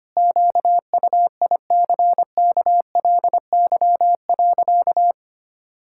Morse code, heard as a keyed tone, about 25 wpm, 700 Hz. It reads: QUICKLY.
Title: Pangram Challenge v2 - 25wpm